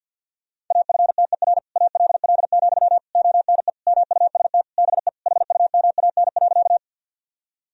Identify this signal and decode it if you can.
Morse code, heard as a keyed tone, about 50 wpm, 700 Hz. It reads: AFTER ALL, ONE MUST BE HUMAN.
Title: Sentences from Top 1000 Words - Review Part 3 - 50wpm